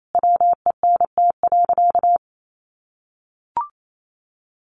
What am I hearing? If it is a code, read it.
Morse code, heard as a keyed tone, about 28 wpm, 700 Hz. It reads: WENT.
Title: Sentences from Top 1300 Words 28wpm